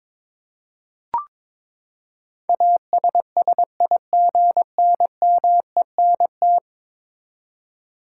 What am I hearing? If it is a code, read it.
Morse code, heard as a keyed tone, about 21 wpm, 700 Hz. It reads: ASSIGNMENT
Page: morse-code-ninja-practice-22wpm